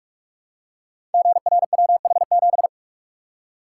Morse code, heard as a keyed tone, about 44 wpm, 700 Hz. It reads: GRWH7
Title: Code Groups - 5 Letter:Numbers - 45wpm